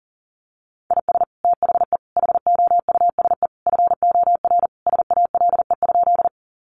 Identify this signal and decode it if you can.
Morse code, heard as a keyed tone, about 40 wpm, 700 Hz. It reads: IS THE HOUSE FOR SALE?
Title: Sentences from Top 1400 Words 40wpm